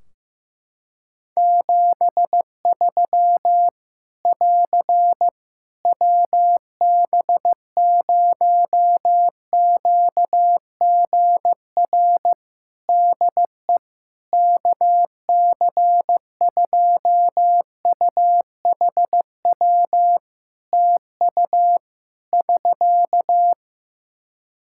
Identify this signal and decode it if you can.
Morse code, heard as a keyed tone, about 16 wpm, 700 Hz. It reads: 73 <AR> WB0QGR DE KC2UHW TU <SK>
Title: Practice QSO - Exchange 4 - Ending - 15wpm